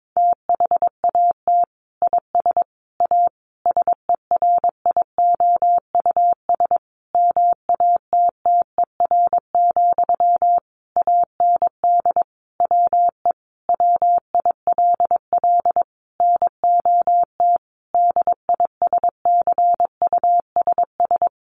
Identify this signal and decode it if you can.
Morse code, heard as a keyed tone, about 22 wpm, 700 Hz. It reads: THAT IS A SERIOUS MATTER, AND WE WILL NOT DISCUSS
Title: Sentences from Top 1100 Words 22wpm